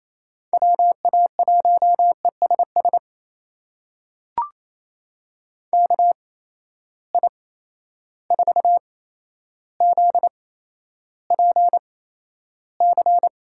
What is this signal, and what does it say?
Morse code, heard as a keyed tone, about 28 wpm, spaced as for 12 wpm, 700 Hz. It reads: WA1ESS KI4ZPC
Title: USA Call Signs - 6 Characters - 8x Character Spacing - 28wpm